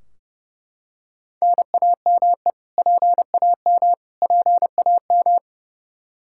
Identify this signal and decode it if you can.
Morse code, heard as a keyed tone, about 31 wpm, 700 Hz. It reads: NAME PAM PAM
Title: Practice QSO - Exchange 1 - Beginning - 30wpm